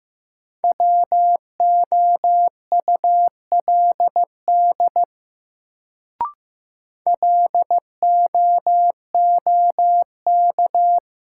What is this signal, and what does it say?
Morse code, heard as a keyed tone, about 15 wpm, 700 Hz. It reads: WOULD LOOK
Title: Sentences from Top 100 Words - New Word Drill - 15wpm